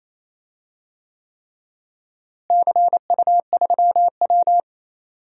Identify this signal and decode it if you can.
Morse code, heard as a keyed tone, about 28 wpm, 700 Hz. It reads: CU3W
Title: DXCC-100 Call Signs - 4 Characters - Repeated 3x - 28wpm